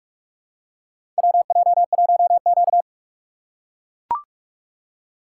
Morse code, heard as a keyed tone, about 44 wpm, 700 Hz. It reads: WJ1Q
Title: USA Call Signs - 4 Characters - 4x Character Spacing - 45wpm